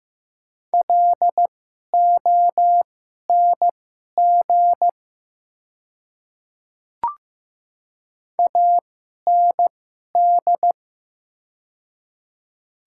Morse code, heard as a keyed tone, about 15 wpm, spaced as for 11 wpm, 700 Hz. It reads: LONG AND
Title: 2x Character Spacing - Top 100 Words - 15wpm